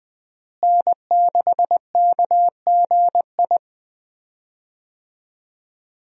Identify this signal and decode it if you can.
Morse code, heard as a keyed tone, about 21 wpm, 700 Hz. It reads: N6KGI
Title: USA Call Signs - 5 Characters - Repeated 2x - 20wpm